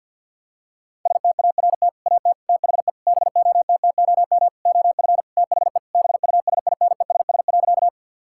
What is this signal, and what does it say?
Morse code, heard as a keyed tone, about 50 wpm, 700 Hz. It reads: START AT THE BOTTOM OF THE BUSINESS.
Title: Sentences from Top 1000 Words - Review Part 6 - 50wpm